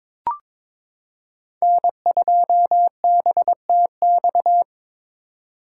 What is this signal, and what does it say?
Morse code, heard as a keyed tone, about 22 wpm, 700 Hz. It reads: N2BTX
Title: K1USN SST QSO Practice - 22wpm